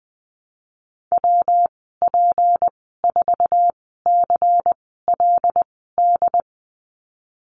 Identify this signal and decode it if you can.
Morse code, heard as a keyed tone, about 20 wpm, spaced as for 15 wpm, 700 Hz. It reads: WP4CLD
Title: 161 Lesson 26g - Letter g - Call Signs 20wpm